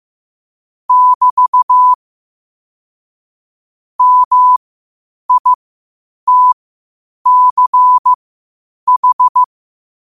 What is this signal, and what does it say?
Morse code, heard as a keyed tone, about 15 wpm, spaced as for 8 wpm, 1000 Hz. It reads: <BT> MITCH
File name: qotdfiles / 8wpm